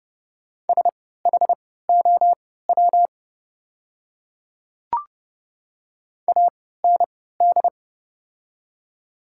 Morse code, heard as a keyed tone, about 30 wpm, spaced as for 17 wpm, 700 Hz. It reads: SHOW AND
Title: 3x Character Spacing - Top 200 Words - 30wpm